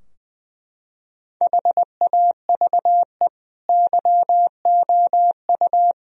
Morse code, heard as a keyed tone, about 20 wpm, 700 Hz. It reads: HAVE YOU
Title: Sentences from Top 1000 Words - Review Part 3 - 20wpm